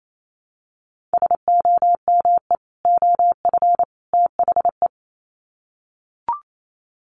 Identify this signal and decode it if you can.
Morse code, heard as a keyed tone, about 28 wpm, 700 Hz. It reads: SOME OF THE
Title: Sets of 3 Words 28wpm